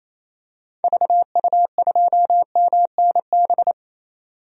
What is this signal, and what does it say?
Morse code, heard as a keyed tone, about 28 wpm, 700 Hz. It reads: VU2MNB
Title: DXCC-100 Call Signs - 6 Characters - 28wpm